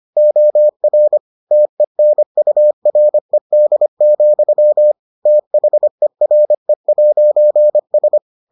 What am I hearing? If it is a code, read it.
Morse code, heard as a keyed tone, about 25 wpm, 600 Hz. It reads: OR TENURED, THERE'S